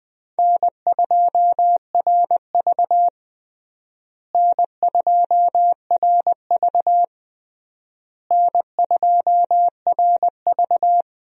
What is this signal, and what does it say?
Morse code, heard as a keyed tone, about 21 wpm, 700 Hz. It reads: N2RV N2RV N2RV
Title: USA Call Signs - 4 Characters - Repeated 3x - 20wpm